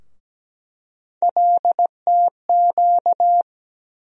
Morse code, heard as a keyed tone, about 17 wpm, 700 Hz. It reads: LTQ
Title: USA Call Sign Postfixes - 3 Letters 17wpm